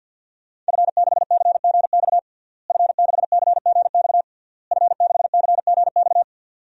Morse code, heard as a keyed tone, about 51 wpm, 700 Hz. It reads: F6CGX F6CGX F6CGX
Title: DXCC-100 Call Signs - 5 Characters - Repeated 3x - 50wpm